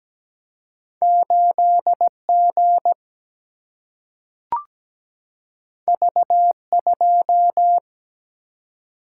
Morse code, heard as a keyed tone, about 17 wpm, 700 Hz. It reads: 8 G V 2
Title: Single-Letter Single-Number 17wpm